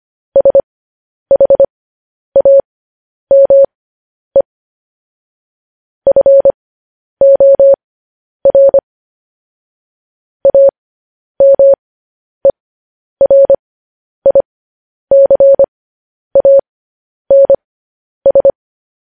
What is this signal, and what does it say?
Morse code, heard as a keyed tone, about 25 wpm, spaced as for 10 wpm, 550 Hz. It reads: SHAME FOR AMERICANS
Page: news headlines in morse code at 10 wpm